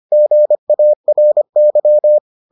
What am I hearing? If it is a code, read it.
Morse code, heard as a keyed tone, about 25 wpm, 600 Hz. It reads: GARY